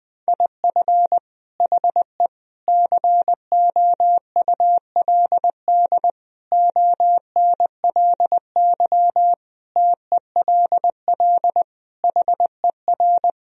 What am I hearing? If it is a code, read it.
Morse code, heard as a keyed tone, about 20 wpm, 700 Hz. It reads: IF HE COULD ONLY TELL HER
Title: Sentences from Top 1000 Words - Review Part 1 - 20wpm